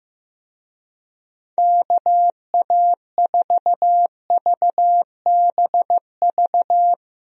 Morse code, heard as a keyed tone, about 15 wpm, 700 Hz. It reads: KA4VBV
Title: USA Call Signs - 6 Characters - 15wpm